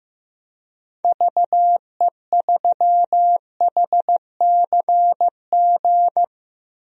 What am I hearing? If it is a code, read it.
Morse code, heard as a keyed tone, about 15 wpm, 700 Hz. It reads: VE3HCG
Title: DXCC-100 Call Signs - 6 Characters - Repeated 3x - 15wpm